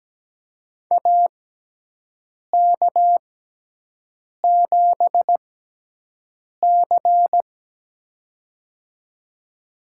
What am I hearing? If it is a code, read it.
Morse code, heard as a keyed tone, about 17 wpm, spaced as for 6 wpm, 700 Hz. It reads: AK7C
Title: USA Call Signs - 6x Character Spacing - 17wpm